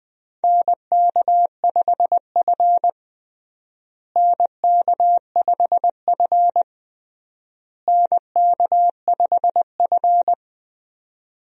Morse code, heard as a keyed tone, about 20 wpm, 700 Hz. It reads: NK5F NK5F NK5F
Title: USA Call Signs - 4 Characters - Repeated 3x - 20wpm